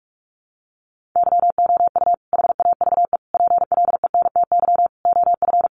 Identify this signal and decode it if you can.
Morse code, heard as a keyed tone, about 44 wpm, 700 Hz. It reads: YOU HAVE PLENTY OF
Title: Sentences from Top 1300 Words 45wpm